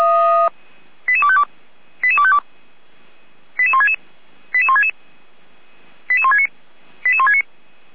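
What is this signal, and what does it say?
Five-tone selective call (ZVEI), heard as quick runs of five tones, tones of about 70 ms. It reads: 88252 88252 88166 88166 88169 88169
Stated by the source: ZVEI five-tone selective call received by SDR, logged by Feuerwehr Orlamünde.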